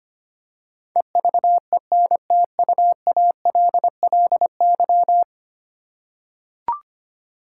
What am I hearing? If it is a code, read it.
Morse code, heard as a keyed tone, about 25 wpm, 700 Hz. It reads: EVENTUALLY
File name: 10 Letter Words - First 3 Letters Given - 25wpm